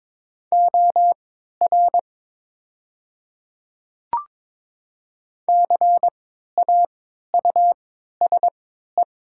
Morse code, heard as a keyed tone, about 22 wpm, spaced as for 12 wpm, 700 Hz. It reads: OR CAUSE
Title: 3x Character Spacing - Top 300 Words - 22wpm